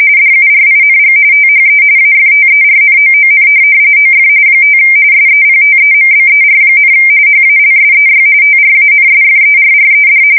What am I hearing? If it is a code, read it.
RTTY, about 45.45 baud. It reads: DURING THE AFTERNOON OF 5 SEPTEMBER, HALSEY TRANSFERRED
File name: fsk170 / USS-Missouri